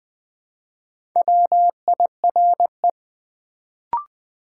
Morse code, heard as a keyed tone, about 20 wpm, 700 Hz. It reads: WIRE
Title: Build A Word - 2 Letters At A Time - 4-Letter Words - 20wpm